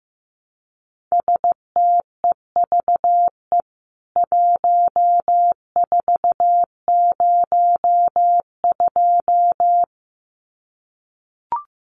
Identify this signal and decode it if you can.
Morse code, heard as a keyed tone, about 15 wpm, 700 Hz. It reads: STEVE 1402
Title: CWT Exchange 15wpm